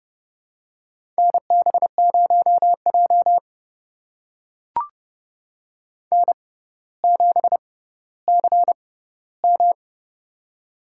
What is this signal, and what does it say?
Morse code, heard as a keyed tone, about 31 wpm, spaced as for 15 wpm, 700 Hz. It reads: NB0J N7CM
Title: USA Call Signs - 4 Characters - 6x Character Spacing - 30wpm